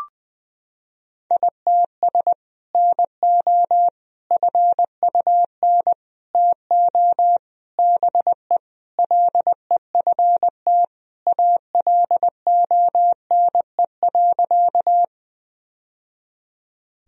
Morse code, heard as a keyed tone, about 20 wpm, 700 Hz. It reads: ITS NO FUN TO BE LEFT ALONE.
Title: Sentences from Top 1000 Words - Review Part 4 - 20wpm